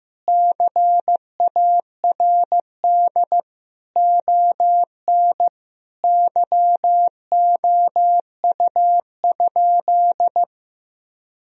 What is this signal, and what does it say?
Morse code, heard as a keyed tone, about 15 wpm, 700 Hz. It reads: CARD ON YOU?
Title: Sentences from Top 1000 Words - Review Part 4 - 15wpm